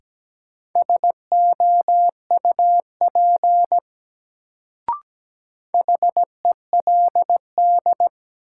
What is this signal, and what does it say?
Morse code, heard as a keyed tone, about 17 wpm, 700 Hz. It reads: SOUP HELD
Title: Sentences from Top 1600 Words - New Word Drill Alt - 17wpm